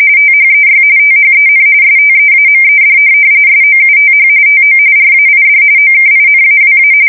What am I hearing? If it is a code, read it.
RTTY, about 45.45 baud. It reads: THEY THEN STEAMED SOUTH TO RAID OKINAWA
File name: fsk170 / USS-Massachusetts